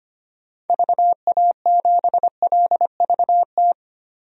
Morse code, heard as a keyed tone, about 25 wpm, 700 Hz. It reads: VA7LVT
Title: DXCC-100 Call Signs - 6 Characters - Repeated 3x - 25wpm